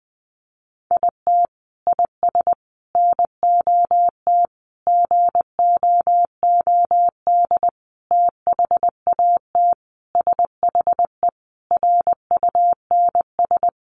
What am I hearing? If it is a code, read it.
Morse code, heard as a keyed tone, about 20 wpm, 700 Hz. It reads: IT IS NOT GOOD THAT SHE RUNS
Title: Sentences from Top 600 Words 20wpm